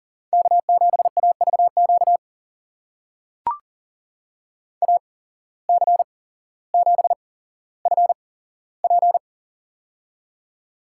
Morse code, heard as a keyed tone, about 40 wpm, spaced as for 17 wpm, 700 Hz. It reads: K7AVQ AC7FP
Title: USA Call Signs - 5 Characters - 8x Character Spacing - 40wpm